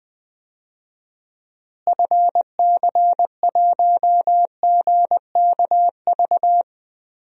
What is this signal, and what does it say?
Morse code, heard as a keyed tone, about 20 wpm, 700 Hz. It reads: FC1GKV
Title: DXCC-100 Call Signs - 6 Characters - Repeated 2x - 20wpm